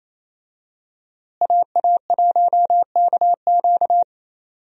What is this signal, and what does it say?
Morse code, heard as a keyed tone, about 28 wpm, 700 Hz. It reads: AA1KQ
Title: USA Call Signs - 5 Characters - Repeated 3x - 28wpm